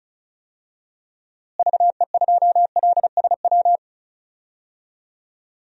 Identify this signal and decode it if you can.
Morse code, heard as a keyed tone, about 34 wpm, 700 Hz. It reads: VE2LSW
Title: DXCC-100 Call Signs - 6 Characters - Repeated 2x - 35wpm